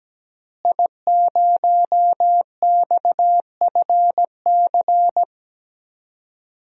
Morse code, heard as a keyed tone, about 17 wpm, 700 Hz. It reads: I0XFC
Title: DXCC-100 Call Signs - 5 Characters - Repeated 3x - 17wpm